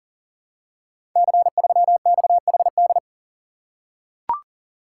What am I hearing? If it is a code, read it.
Morse code, heard as a keyed tone, about 39 wpm, 690 Hz. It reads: C3XHD
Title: Code Groups - 5 Letter:Numbers - 40wpm